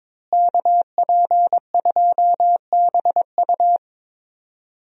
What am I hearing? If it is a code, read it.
Morse code, heard as a keyed tone, about 22 wpm, 700 Hz. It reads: KP2BU
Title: USA Call Signs - 5 Characters - Repeated 3x - 22wpm